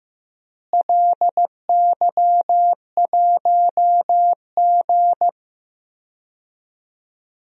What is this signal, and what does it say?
Morse code, heard as a keyed tone, about 15 wpm, 700 Hz. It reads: LY1G
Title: DXCC-100 Call Signs - 4 Characters - Repeated 3x - 15wpm